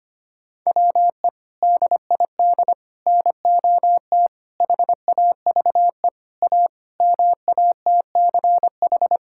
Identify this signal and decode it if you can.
Morse code, heard as a keyed tone, about 25 wpm, 700 Hz. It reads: WE DID NOT HAVE A MATCH
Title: Sentences from Top 1600 Words 25wpm